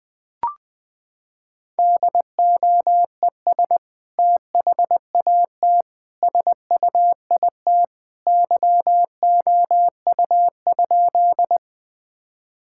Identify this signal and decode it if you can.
Morse code, heard as a keyed tone, about 20 wpm, 700 Hz. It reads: DOES THAT SUIT YOU?
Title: Sentences from Top 1600 Words 20wpm